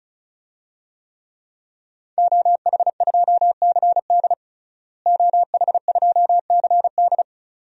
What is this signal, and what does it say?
Morse code, heard as a keyed tone, about 36 wpm, 700 Hz. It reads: OH2CD OH2CD
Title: DXCC-100 Call Signs - 5 Characters - Repeated 2x - 35wpm